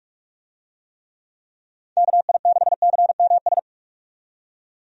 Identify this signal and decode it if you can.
Morse code, heard as a keyed tone, about 45 wpm, 700 Hz. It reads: KI6CMS
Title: USA Call Signs - 6 Characters - 45wpm